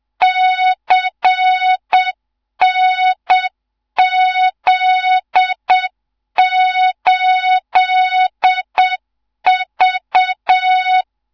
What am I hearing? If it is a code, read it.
Morse code, heard as a keyed tone, about 7 wpm, 3700 Hz. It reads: CNZ8V
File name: Morze-HA2MN / hang / 15 GYAKORLAT